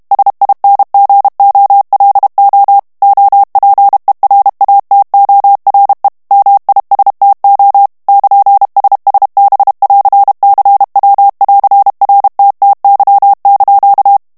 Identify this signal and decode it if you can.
Morse code, heard as a keyed tone, about 32 wpm, 800 Hz. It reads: SINGOLO OPERATORE MISTO <KN>SSB<AR>CW<AR>RTTY)
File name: texte clair 2 vitesse 30 mots minute